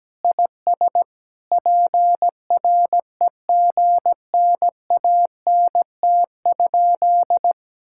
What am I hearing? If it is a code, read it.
Morse code, heard as a keyed tone, about 17 wpm, 700 Hz. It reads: IS PREGNANT?
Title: Sentences from Top 1000 Words - Review Part 6 - 17wpm